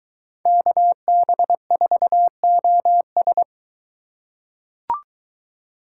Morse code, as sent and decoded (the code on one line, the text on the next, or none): -.- -... ....- --- ...
KB4OS